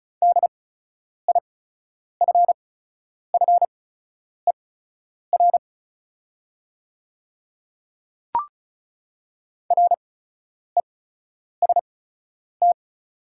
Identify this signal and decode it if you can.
Morse code, heard as a keyed tone, about 34 wpm, spaced as for 10 wpm, 690 Hz. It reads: DIFFER REST
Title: 8x Character Spacing - Top 500 Words - 35wpm